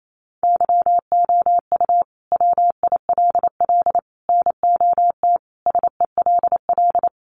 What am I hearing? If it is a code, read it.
Morse code, heard as a keyed tone, about 28 wpm, 700 Hz. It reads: YOU WILL NOT SELL
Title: Sentences from Top 1100 Words 28wpm